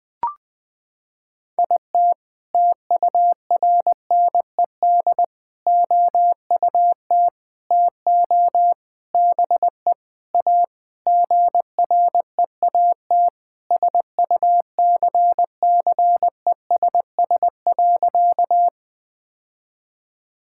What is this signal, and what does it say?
Morse code, heard as a keyed tone, about 20 wpm, 700 Hz. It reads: IT TURNED OUT TO BE A GREAT SUCCESS.
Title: Sentences with 6-Word Phrases - 20wpm